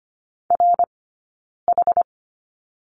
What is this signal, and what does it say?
Morse code, heard as a keyed tone, about 25 wpm, spaced as for 9 wpm, 690 Hz. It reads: RH
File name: course / 25:10wpm / 092 Lesson 16d - Letter c - Two Characters 25wpm